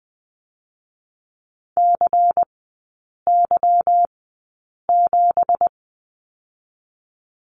Morse code, heard as a keyed tone, about 20 wpm, spaced as for 8 wpm, 700 Hz. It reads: CY7
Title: 166 Lesson 27e - Number 7 - Three Characters 20wpm